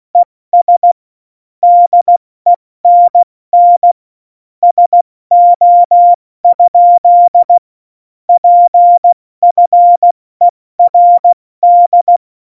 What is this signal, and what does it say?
Morse code, heard as a keyed tone, about 16 wpm, spaced as for 14 wpm, 700 Hz. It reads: ES DENN SO? PFERD
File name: Morsekurs / Witze / Witz 06. Oktober 2020, 70 F80